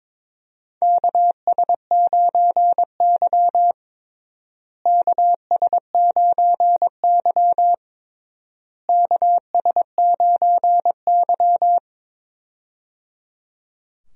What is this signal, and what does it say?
Morse code, heard as a keyed tone, about 22 wpm, 700 Hz. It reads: KS9Y KS9Y KS9Y
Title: USA Call Signs - 4 Characters - Repeated 3x - 22wpm